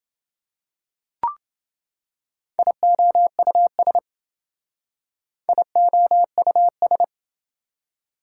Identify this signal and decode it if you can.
Morse code, heard as a keyed tone, about 27 wpm, 700 Hz. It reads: IOUS IOUS